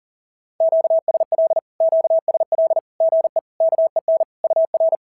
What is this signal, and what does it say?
Morse code, heard as a keyed tone, about 40 wpm, 660 Hz. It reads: QSL QSL GE KEN UR
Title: SOTA QSO Practice 40wpm